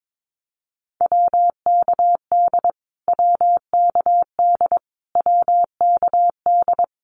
Morse code, heard as a keyed tone, about 22 wpm, 700 Hz. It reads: WKD WKD WKD
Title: Abbreviations Repeated 3x - 22wpm